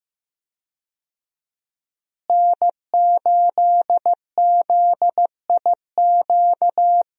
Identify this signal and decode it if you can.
Morse code, heard as a keyed tone, about 15 wpm, 700 Hz. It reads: N8ZIQ
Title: USA Call Signs - 5 Characters - Repeated 2x - 15wpm